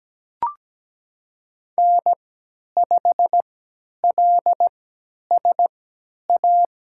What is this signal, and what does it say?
Morse code, heard as a keyed tone, about 17 wpm, spaced as for 10 wpm, 700 Hz. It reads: N5LSA
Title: USA Call Signs - 5 Characters - 3x Character Spacing - 17wpm